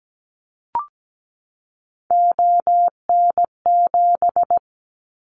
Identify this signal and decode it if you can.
Morse code, heard as a keyed tone, about 17 wpm, 700 Hz. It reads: ON7
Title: Callsign Prefixes for Top 100 DXCC - 17wpm